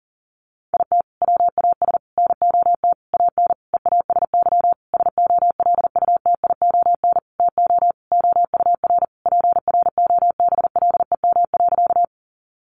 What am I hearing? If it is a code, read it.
Morse code, heard as a keyed tone, about 40 wpm, 700 Hz. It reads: IT WAS NOT AN EASY SOLUTION TO OUR PROBLEM.
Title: Sentences from Top 1300 Words 40wpm